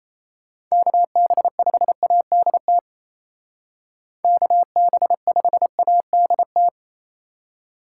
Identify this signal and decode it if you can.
Morse code, heard as a keyed tone, about 31 wpm, 700 Hz. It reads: KB5ADT KB5ADT
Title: Speed Racing - USA 2x3 Call Signs 28wpm